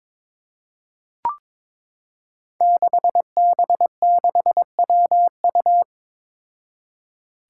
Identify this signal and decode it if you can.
Morse code, heard as a keyed tone, about 22 wpm, 700 Hz. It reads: 6B6WU
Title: Five Challenging Letter-Number Combinations 22wpm